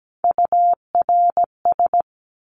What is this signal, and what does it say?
Morse code, heard as a keyed tone, about 17 wpm, 700 Hz. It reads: URS
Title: Abbreviations 17wpm